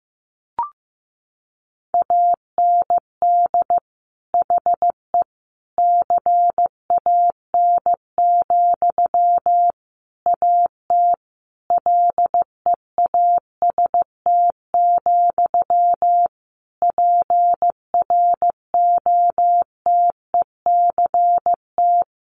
Morse code, heard as a keyed tone, about 15 wpm, 700 Hz. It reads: AND HE CAN, AT LEAST, PROTECT
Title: Sentences from Top 900 Words 15wpm